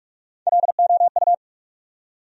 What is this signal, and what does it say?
Morse code, heard as a keyed tone, about 46 wpm, 700 Hz. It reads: LOU
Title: USA Common Names - Max 3 Letters - 45wpm